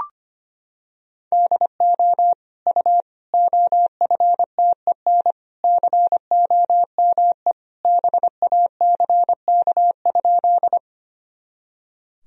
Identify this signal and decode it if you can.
Morse code, heard as a keyed tone, about 25 wpm, 700 Hz. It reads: DO U OFTEN COME BACK?
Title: Sentences from Top 300 Words and Abbreviations 25wpm